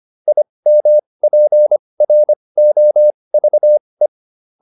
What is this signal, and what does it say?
Morse code, heard as a keyed tone, about 25 wpm, spaced as for 20 wpm, 600 Hz. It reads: IMPROVE